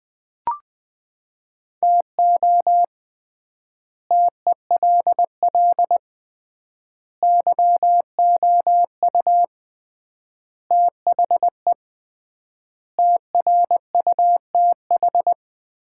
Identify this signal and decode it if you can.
Morse code, heard as a keyed tone, about 20 wpm, 700 Hz. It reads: TO TELL YOU THE TRUTH
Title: 3x Word-Spacing - Sets of 5 Words - 20wpm